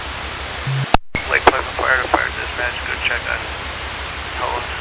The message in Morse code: none

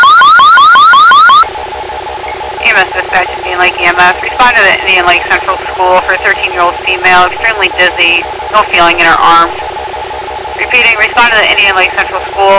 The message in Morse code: none

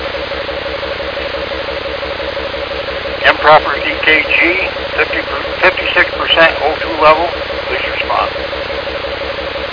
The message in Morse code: none

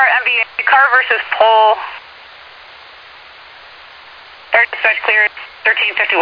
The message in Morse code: none

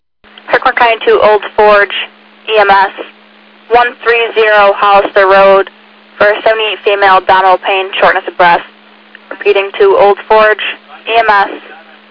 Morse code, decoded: none

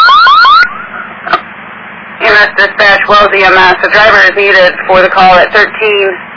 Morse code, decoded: none

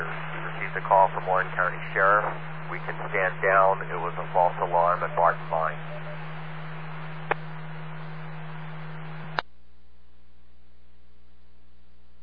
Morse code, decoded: none

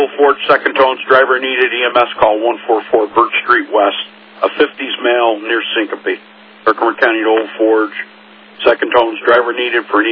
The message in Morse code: none

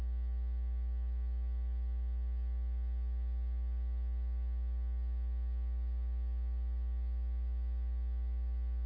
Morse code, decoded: none

